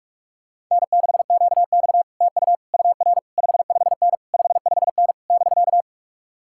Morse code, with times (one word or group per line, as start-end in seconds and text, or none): 0.7-2.0 N6QX
2.2-2.6 TU
2.7-3.2 UR
3.4-4.2 55N
4.3-5.1 55N
5.3-5.8 <BK>